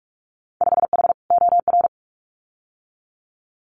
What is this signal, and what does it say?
Morse code, heard as a keyed tone, about 46 wpm, 700 Hz. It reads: 5H OR
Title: Winter Field Day Exchanges Encore 45wpm